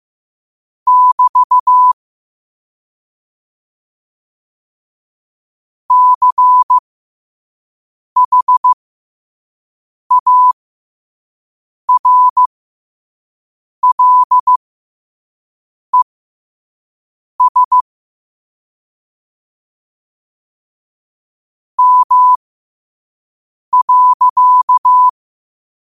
Morse code, heard as a keyed tone, about 15 wpm, spaced as for 5 wpm, 1000 Hz. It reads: <BT> CHARLES M.
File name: qotdfiles / 5wpm